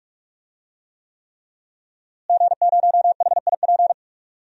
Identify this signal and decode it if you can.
Morse code, heard as a keyed tone, about 47 wpm, 700 Hz. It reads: G0HIP
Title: DXCC-100 Call Signs - 5 Characters - Repeated 3x - 45wpm